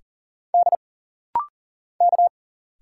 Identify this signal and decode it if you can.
Morse code, heard as a keyed tone, about 40 wpm, 700 Hz. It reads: D K
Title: Single Letters - Rapid-Fire 40wpm